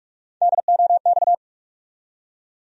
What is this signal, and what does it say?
Morse code, heard as a keyed tone, about 41 wpm, 700 Hz. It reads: DOX
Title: USA Call Sign Postfixes - 3 Letters 45wpm